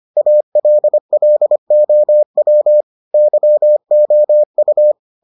Morse code, heard as a keyed tone, about 25 wpm, 600 Hz. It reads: ALLOW YOU